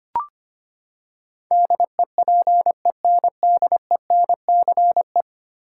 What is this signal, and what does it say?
Morse code, heard as a keyed tone, about 25 wpm, 700 Hz. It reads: DEPENDENCE
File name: Ten Letter Words - 25wpm